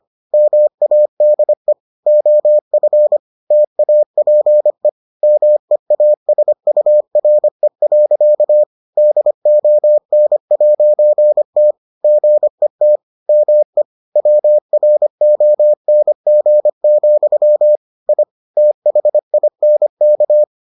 Morse code, heard as a keyed tone, about 25 wpm, 600 Hz. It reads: MADE OF TAPE MEASURE. DON'T GET ME WRONG, I THINK